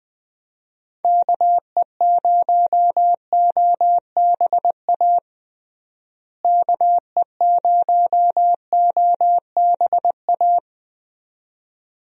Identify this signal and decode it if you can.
Morse code, heard as a keyed tone, about 20 wpm, 700 Hz. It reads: KE0OBA KE0OBA
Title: USA Call Signs - 6 Characters - Repeated 2x - 20wpm